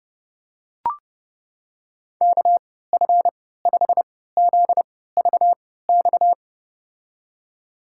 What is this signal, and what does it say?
Morse code, heard as a keyed tone, about 30 wpm, spaced as for 17 wpm, 700 Hz. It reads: KF5ZVX
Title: USA Call Signs - 3x Character Spacing - 30wpm